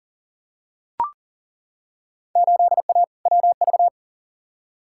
Morse code, heard as a keyed tone, about 40 wpm, 700 Hz. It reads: 8A WV
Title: Field Day Exchanges Encore - 40wpm